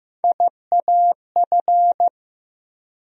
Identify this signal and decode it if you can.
Morse code, heard as a keyed tone, about 15 wpm, 700 Hz. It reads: IAF
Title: USA Call Sign Postfixes - 3 Letters 15wpm